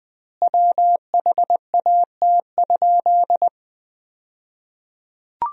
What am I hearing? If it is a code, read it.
Morse code, heard as a keyed tone, about 21 wpm, 700 Hz. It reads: WHAT?
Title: 2x Word-Spacing - Sentences from Top 500 Words - 20wpm